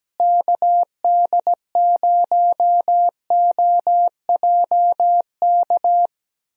Morse code, heard as a keyed tone, about 17 wpm, 700 Hz. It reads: KD0OJK
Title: USA Call Signs - 6 Characters - Repeated 3x - 17wpm